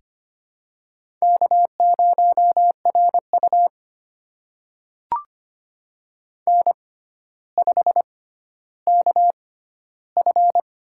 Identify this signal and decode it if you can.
Morse code, heard as a keyed tone, about 25 wpm, spaced as for 13 wpm, 700 Hz. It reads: K0RU N5KF
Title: USA Call Signs - 6x Character Spacing - 25wpm